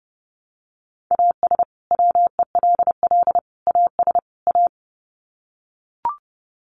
Morse code, heard as a keyed tone, about 30 wpm, 700 Hz. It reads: AS WELL AS A
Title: Sets of 4 Words 30wpm